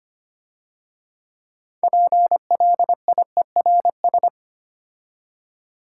Morse code, heard as a keyed tone, about 30 wpm, 690 Hz. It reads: PLIERS
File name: Easy Words - Speed-Racing with Hints - 25wpm